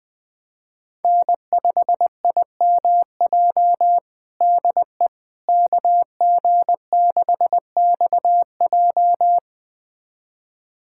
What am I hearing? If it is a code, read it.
Morse code, heard as a keyed tone, about 20 wpm, 700 Hz. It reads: N5IMJ DE KG6XJ
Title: USA Callsign DE Callsign - 5 Characters - 20wpm